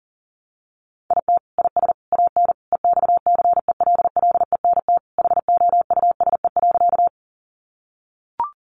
Morse code, heard as a keyed tone, about 37 wpm, 700 Hz. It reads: IT IS AN EXCELLENT HOUSE.
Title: Sentences from Top 1000 Words 40wpm